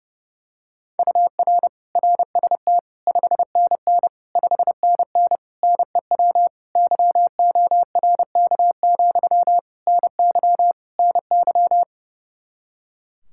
Morse code, heard as a keyed tone, about 31 wpm, 700 Hz. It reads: UR RST 5NN 5NN NEW YORK, NY NY
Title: Practice QSO - Exchange 1 - Beginning - 30wpm